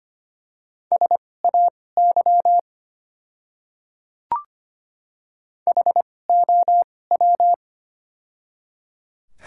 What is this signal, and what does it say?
Morse code, heard as a keyed tone, about 25 wpm, spaced as for 18 wpm, 700 Hz. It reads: SAY HOW
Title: 2x Character Spacing - Top 200 Words - 25wpm